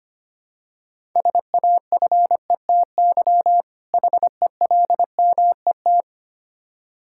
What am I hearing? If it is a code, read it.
Morse code, heard as a keyed tone, about 25 wpm, 700 Hz. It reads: SAFETY HELMET
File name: practice / 25wpm / Difficult Words with Hints - 25wpm